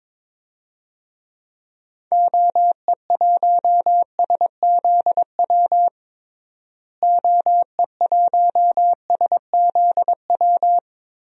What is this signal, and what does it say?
Morse code, heard as a keyed tone, about 22 wpm, 700 Hz. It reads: OE1SZW OE1SZW
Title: DXCC-100 Call Signs - 6 Characters - Repeated 3x - 22wpm